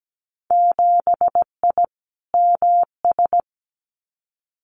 Morse code, heard as a keyed tone, about 17 wpm, 700 Hz. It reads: 7I MS
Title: Winter Field Day Exchanges Encore 17wpm